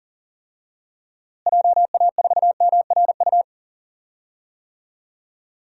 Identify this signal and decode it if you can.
Morse code, heard as a keyed tone, about 40 wpm, 700 Hz. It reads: JA4MRU
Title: DXCC-100 Call Signs - 6 Characters - Repeated 2x - 40wpm